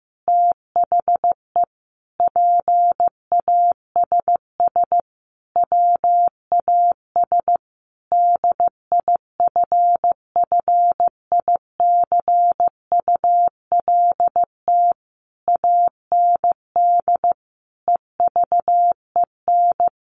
Morse code, heard as a keyed tone, about 14 wpm, 700 Hz. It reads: THE PASS WAS DIFFICULT AND EVEN
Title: Sentences from Top 1200 Words 15wpm